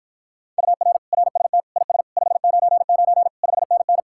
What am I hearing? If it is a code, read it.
Morse code, heard as a keyed tone, about 53 wpm, 700 Hz. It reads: UR RST IS 599 5NN